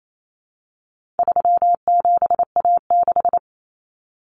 Morse code, heard as a keyed tone, about 28 wpm, 700 Hz. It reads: 37A6
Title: Common Cut Numbers - 4-Digit 28wpm